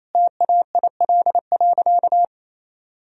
Morse code, heard as a keyed tone, about 28 wpm, 700 Hz. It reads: TAIL.
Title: Sentences from Top 1000 Words - Review Part 3 - 28wpm